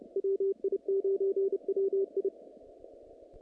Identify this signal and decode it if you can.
Morse code, heard as a keyed tone, about 30 wpm, 390 Hz. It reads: WI9WI
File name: contests / naqp-cw-k5tr-op-k5pi- -2100